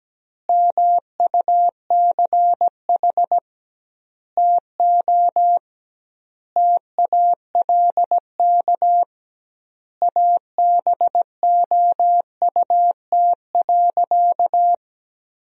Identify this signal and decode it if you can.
Morse code, heard as a keyed tone, about 17 wpm, 700 Hz. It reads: MUCH TO TALK ABOUT.
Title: 2x Word-Spacing - Sentences from Top 400 Words - 17wpm